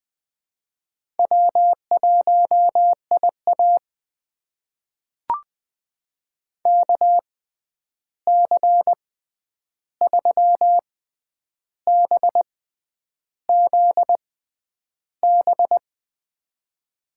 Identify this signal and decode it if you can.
Morse code, heard as a keyed tone, about 21 wpm, spaced as for 9 wpm, 700 Hz. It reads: W1IA KC3BZB
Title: USA Call Signs - 6x Character Spacing - 20wpm